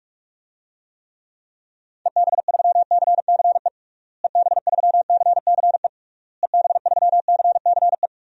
Morse code, heard as a keyed tone, about 42 wpm, 700 Hz. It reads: EB3CCE EB3CCE EB3CCE
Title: DXCC-100 Call Signs - 6 Characters - Repeated 3x - 45wpm